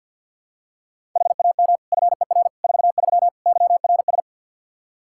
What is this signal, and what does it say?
Morse code, heard as a keyed tone, about 52 wpm, 700 Hz. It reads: HAM FER 43 YRS
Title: Practice QSO - Exchange 3 - More Info - 50wpm